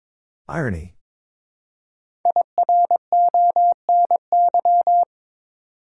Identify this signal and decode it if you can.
Morse code, heard as a keyed tone, about 22 wpm, 700 Hz. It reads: IRONY
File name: 5 Letter Words - First 3 Letters Given - 22wpm